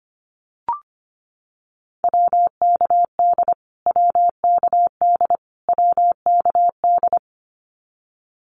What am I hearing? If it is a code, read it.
Morse code, heard as a keyed tone, about 25 wpm, 700 Hz. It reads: WKD WKD WKD
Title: Abbreviations Repeated 3x - 25wpm